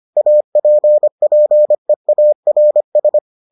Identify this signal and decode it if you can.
Morse code, heard as a keyed tone, about 25 wpm, 600 Hz. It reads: APPEARS